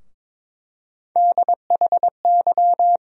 Morse code, heard as a keyed tone, about 22 wpm, 700 Hz. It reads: DHY
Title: USA Call Sign Postfixes - 3 Letters 22wpm